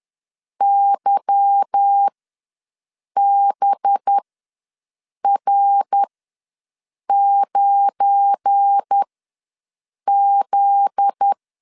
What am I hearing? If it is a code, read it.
Morse code, heard as a keyed tone, about 10 wpm, spaced as for 6 wpm, 800 Hz. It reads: YBR9Z
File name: LEZIONI / 8 YJ45 / 8 YJ ws 10wpm 4043